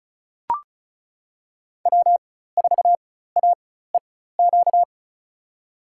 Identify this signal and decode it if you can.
Morse code, heard as a keyed tone, about 36 wpm, spaced as for 16 wpm, 700 Hz. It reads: W4AEQ
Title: USA Call Signs - 5 Characters - 4x Character Spacing - 35wpm